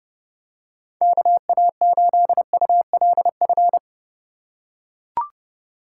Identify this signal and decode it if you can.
Morse code, heard as a keyed tone, about 32 wpm, 700 Hz. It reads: KA8ULF
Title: USA Call Signs - 4x Character Spacing - 30wpm